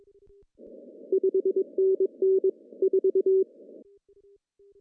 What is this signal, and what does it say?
Morse code, heard as a keyed tone, about 20 wpm, 410 Hz. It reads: 5NN 4
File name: CW / cqww-cw-k5tr